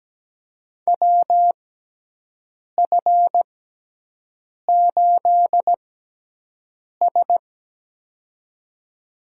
Morse code, heard as a keyed tone, about 17 wpm, spaced as for 6 wpm, 700 Hz. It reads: WF8S
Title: USA Call Signs - 4 Characters - 6x Character Spacing - 17wpm